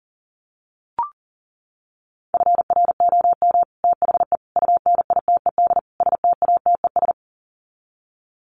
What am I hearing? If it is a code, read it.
Morse code, heard as a keyed tone, about 40 wpm, 700 Hz. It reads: FROM THE UNITED STATES
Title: Sets of 4 Words 40wpm